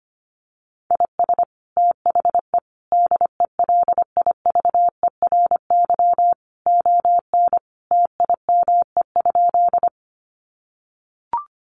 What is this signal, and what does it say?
Morse code, heard as a keyed tone, about 25 wpm, 700 Hz. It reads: IS THE DELIVERY ON TIME?
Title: Sentences from Top 1500 Words 25wpm